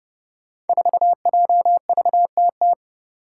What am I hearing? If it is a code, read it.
Morse code, heard as a keyed tone, about 30 wpm, 700 Hz. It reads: 4JVTT
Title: Five Challenging Letter-Number Combinations 30wpm